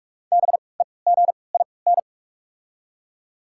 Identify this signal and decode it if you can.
Morse code, heard as a keyed tone, about 44 wpm, spaced as for 25 wpm, 690 Hz. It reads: BEGIN
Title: 3x Character Spacing - Top 300 Words - 45wpm